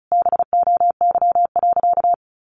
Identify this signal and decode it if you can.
Morse code, heard as a keyed tone, about 34 wpm, 700 Hz. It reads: BOY.
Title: Sentences from Top 1400 Words 35wpm